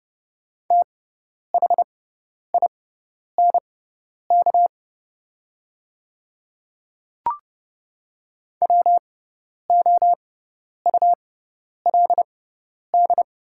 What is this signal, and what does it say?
Morse code, heard as a keyed tone, about 30 wpm, spaced as for 10 wpm, 700 Hz. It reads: THINK WOULD
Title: 6x Character Spacing - Top 200 Words - 30wpm